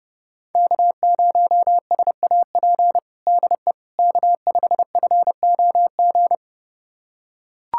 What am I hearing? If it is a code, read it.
Morse code, heard as a keyed tone, about 30 wpm, 700 Hz. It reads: K0SAP DE K5FOG
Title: USA Callsign DE Callsign - 5 Characters - 30wpm